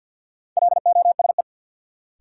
Morse code, heard as a keyed tone, about 49 wpm, 700 Hz. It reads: LOSE
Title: Sentences from Top 1100 Words - New Word Drill - 50wpm